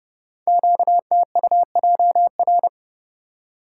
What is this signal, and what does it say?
Morse code, heard as a keyed tone, about 30 wpm, 700 Hz. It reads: QTUJR
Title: Code Groups - 5 Letters - 30wpm